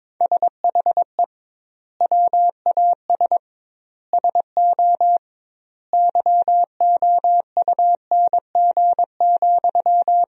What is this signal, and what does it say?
Morse code, heard as a keyed tone, about 22 wpm, 700 Hz. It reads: SHE WAS SO YOUNG,
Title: 2x Word-Spacing - Sentences from Top 400 Words - 22wpm